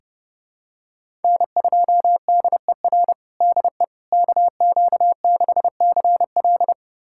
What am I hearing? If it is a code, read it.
Morse code, heard as a keyed tone, about 32 wpm, 700 Hz. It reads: N2DER DE KQ6CL
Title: USA Callsign DE Callsign - 5 Characters - 30wpm